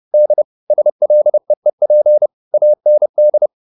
Morse code, heard as a keyed tone, about 30 wpm, 600 Hz. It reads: D SLEEP AND
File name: the old man and the sea / the old man and the sea 1 30wpm